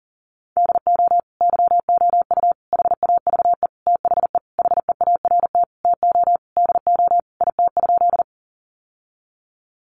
Morse code, heard as a keyed tone, about 40 wpm, 700 Hz. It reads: DO YOU HAVE THE HEART TO DO IT?
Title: Sentences from Top 1200 Words 40wpm